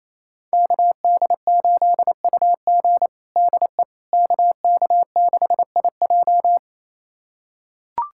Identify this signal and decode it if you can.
Morse code, heard as a keyed tone, about 28 wpm, 700 Hz. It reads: KD8UG DE KK6IJ
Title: USA Callsign DE Callsign - 5 Characters - 28wpm